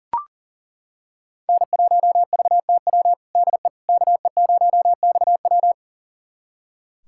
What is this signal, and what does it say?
Morse code, heard as a keyed tone, about 40 wpm, 700 Hz. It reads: N1VTW DE KE0XW
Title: USA Callsign DE Callsign - 5 Characters - 40wpm